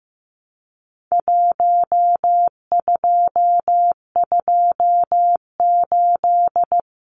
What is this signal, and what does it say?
Morse code, heard as a keyed tone, about 15 wpm, 700 Hz. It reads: 1228
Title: Numbers - Contest Style - 4-Digit 15wpm